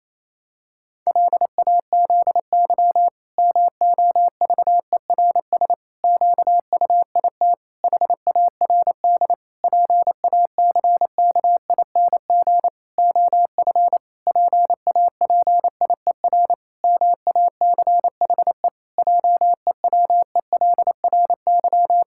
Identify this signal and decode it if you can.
Morse code, heard as a keyed tone, about 28 wpm, 700 Hz. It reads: LAZY MOVERS QUIT HARD PACKING OF PAPIER MACHE JEWELRY
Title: Pangram Challenge v2 - 28wpm